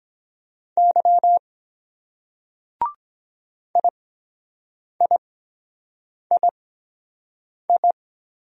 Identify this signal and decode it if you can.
Morse code, heard as a keyed tone, about 23 wpm, spaced as for 7 wpm, 700 Hz. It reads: Y IIII